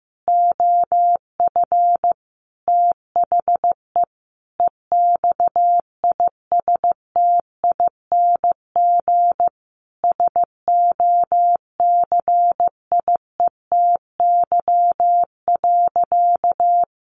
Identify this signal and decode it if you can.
Morse code, heard as a keyed tone, about 15 wpm, 700 Hz. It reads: OF THE EXISTING SOCIETY.
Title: Sentences from Top 1500 Words 15wpm